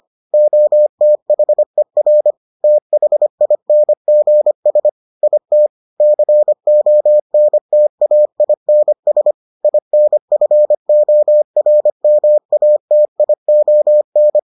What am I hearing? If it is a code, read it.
Morse code, heard as a keyed tone, about 25 wpm, 600 Hz. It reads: OTHER THINGS IT CONTAINS INFORMATION